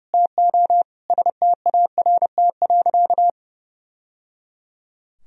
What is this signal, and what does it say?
Morse code, heard as a keyed tone, about 30 wpm, 700 Hz. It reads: TO START.
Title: Sentences from Top 1000 Words - Review Part 4 - 30wpm